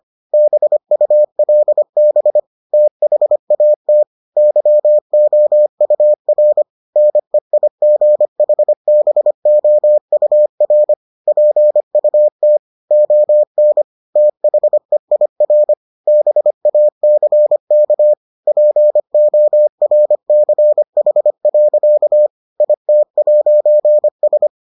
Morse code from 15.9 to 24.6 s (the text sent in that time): BACK PORCH. IT'S